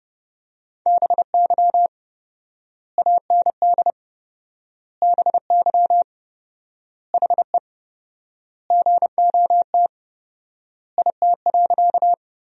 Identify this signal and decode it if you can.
Morse code, heard as a keyed tone, about 30 wpm, 700 Hz. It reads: BY AND BY HE GOT IT.
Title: 4x Word-Spacing - Sentences from Top 300 Words - 30wpm